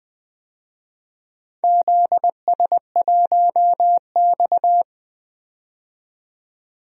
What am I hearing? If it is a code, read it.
Morse code, heard as a keyed tone, about 20 wpm, 700 Hz. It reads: ZS1X
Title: DXCC-100 Call Signs - 4 Characters - 20wpm